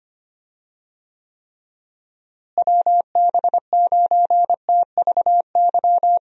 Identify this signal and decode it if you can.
Morse code, heard as a keyed tone, about 25 wpm, 700 Hz. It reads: WB9TVY
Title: USA Call Signs - 6 Characters - Repeated 2x - 25wpm